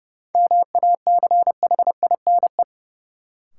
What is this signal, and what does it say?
Morse code, heard as a keyed tone, about 30 wpm, 700 Hz. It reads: MACHINE
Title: Sentences from Top 1000 Words - Review Part 2 - New Word Drill - 30wpm